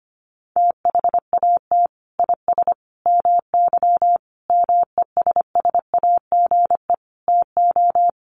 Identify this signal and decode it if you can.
Morse code, heard as a keyed tone, about 25 wpm, 700 Hz. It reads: THAT IS MY MESSAGE TO
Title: Sentences from Top 1100 Words 25wpm